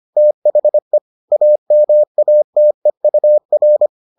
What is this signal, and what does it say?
Morse code, heard as a keyed tone, about 25 wpm, 600 Hz. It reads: THE AMATEUR